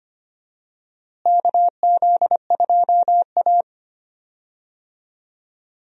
Morse code, heard as a keyed tone, about 25 wpm, 700 Hz. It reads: KZ2A
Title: USA Call Signs - 4 Characters - Repeated 3x - 25wpm